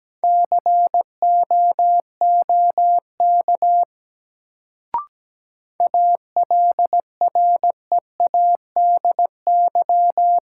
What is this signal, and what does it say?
Morse code, heard as a keyed tone, about 17 wpm, 700 Hz. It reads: COOK ALREADY
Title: Sentences from Top 700 Words - New Word Drill - 17wpm